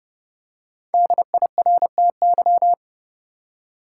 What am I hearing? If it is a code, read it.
Morse code, heard as a keyed tone, about 30 wpm, 700 Hz. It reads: DIRTY
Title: Five Letter Words - 30wpm